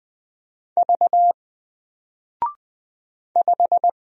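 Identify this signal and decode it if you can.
Morse code, heard as a keyed tone, about 20 wpm, 700 Hz. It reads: V 5
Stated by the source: Single letter and number Morse drill at 20 wpm, from N4LCD.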